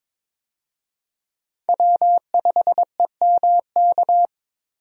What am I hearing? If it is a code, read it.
Morse code, heard as a keyed tone, about 22 wpm, 700 Hz. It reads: W5EMK
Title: USA Call Signs - 5 Characters - 22wpm